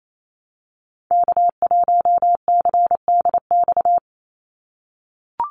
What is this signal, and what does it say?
Morse code, heard as a keyed tone, about 28 wpm, 700 Hz. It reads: K1CDX
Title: Call Signs - 28wpm